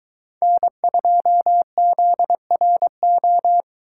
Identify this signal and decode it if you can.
Morse code, heard as a keyed tone, about 23 wpm, 700 Hz. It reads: N2ZRO